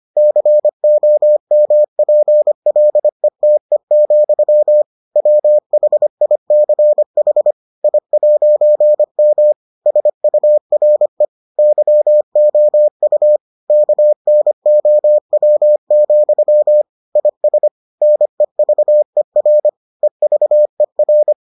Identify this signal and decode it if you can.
Morse code, heard as a keyed tone, about 25 wpm, 600 Hz. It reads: COMPLETE, WHICH I'M SURE YOU KNOW, IS NEVER EVER